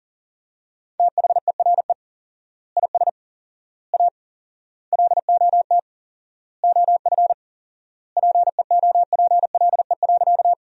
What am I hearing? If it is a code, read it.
Morse code, heard as a keyed tone, about 37 wpm, 700 Hz. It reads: THERE IS A LOT OF PEOPLE.
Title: 4x Word-Spacing - Sentences from Top 500 Words - 40wpm